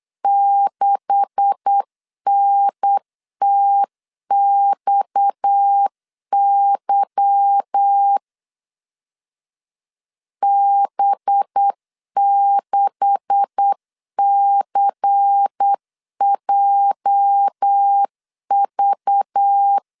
Morse code, heard as a keyed tone, about 8 wpm, 800 Hz. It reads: 6NTXY B6CJV
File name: LEZIONI / 9 286 8wpm 3425